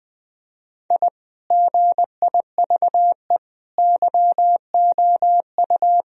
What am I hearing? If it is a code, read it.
Morse code, heard as a keyed tone, about 21 wpm, 700 Hz. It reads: I GIVE YOU
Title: Sentences from Top 1000 Words - Review Part 2 - 20wpm